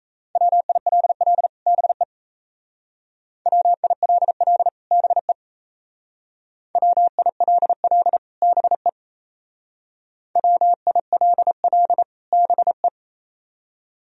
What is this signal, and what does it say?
Morse code, heard as a keyed tone, about 37 wpm, 700 Hz. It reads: WILL BE WILL BE WILL BE WILL BE